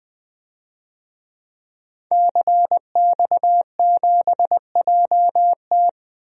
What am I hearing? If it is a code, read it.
Morse code, heard as a keyed tone, about 20 wpm, 700 Hz. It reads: CX7JT
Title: DXCC-100 Call Signs - 5 Characters - 20wpm